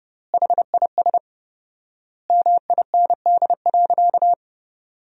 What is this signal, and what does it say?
Morse code, heard as a keyed tone, about 30 wpm, 700 Hz. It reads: HIS MIND.
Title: 4x Word-Spacing - Sentences from Top 500 Words - 30wpm